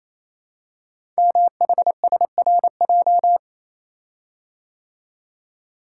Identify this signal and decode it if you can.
Morse code, heard as a keyed tone, about 27 wpm, 700 Hz. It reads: MHSRJ
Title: Code Groups - 5 Letters - 28wpm